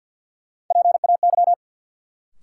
Morse code, heard as a keyed tone, about 47 wpm, 700 Hz. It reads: PAY
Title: Sentences from Top 700 Words - New Word Drill - 50wpm